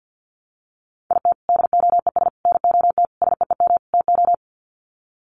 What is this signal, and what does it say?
Morse code, heard as a keyed tone, about 48 wpm, 700 Hz. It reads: IT DOES NOT SEEM TO
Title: Sets of 5 Words Encore2 50wpm